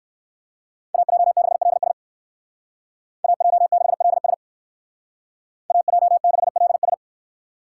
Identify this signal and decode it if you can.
Morse code, heard as a keyed tone, about 54 wpm, 700 Hz. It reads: AJ6LS AJ6LS AJ6LS
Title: Speed Racing - USA 2x2 Call Signs 45wpm